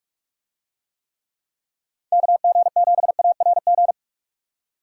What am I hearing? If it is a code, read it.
Morse code, heard as a keyed tone, about 42 wpm, 700 Hz. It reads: KG7ARG